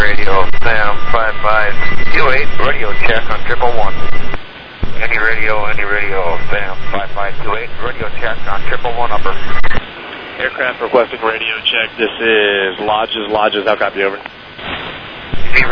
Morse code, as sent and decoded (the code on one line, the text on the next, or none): none